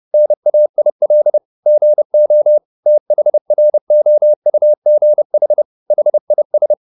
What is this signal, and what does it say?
Morse code, heard as a keyed tone, about 30 wpm, 600 Hz. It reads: NAIL GO THROUGH HIS